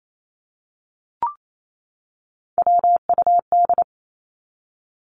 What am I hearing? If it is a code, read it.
Morse code, heard as a keyed tone, about 28 wpm, 700 Hz. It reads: WUD
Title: Abbreviations 28wpm